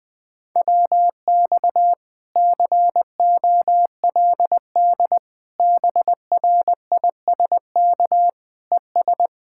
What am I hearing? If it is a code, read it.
Morse code, heard as a keyed tone, about 20 wpm, 700 Hz. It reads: WX COLD BRISK ES